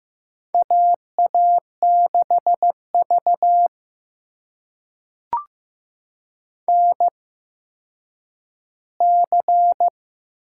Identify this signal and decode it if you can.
Morse code, heard as a keyed tone, about 15 wpm, spaced as for 9 wpm, 700 Hz. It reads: AA6V NC
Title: USA Call Signs - 4 Characters - 8x Character Spacing - 15wpm